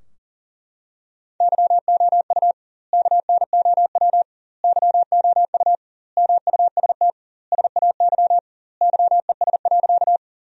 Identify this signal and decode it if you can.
Morse code, heard as a keyed tone, about 40 wpm, 700 Hz. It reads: YOU KNOW YOU MUST SAY YES.
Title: 2x Word-Spacing - Sentences from Top 500 Words - 40wpm